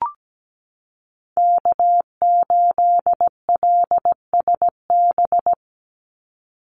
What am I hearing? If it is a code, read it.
Morse code, heard as a keyed tone, about 17 wpm, 700 Hz. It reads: K8LSB
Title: Call Signs Repeated 3x - 17wpm